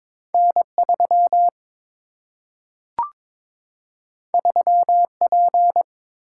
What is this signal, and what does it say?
Morse code, heard as a keyed tone, about 22 wpm, 700 Hz. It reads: N3 3P
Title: Single-Letter Single-Number 22wpm